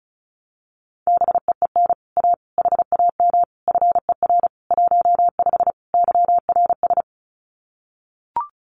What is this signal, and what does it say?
Morse code, heard as a keyed tone, about 34 wpm, 700 Hz. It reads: BEEN A HAM FER 15 YRS
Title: Common QSO Phrases Encore 35wpm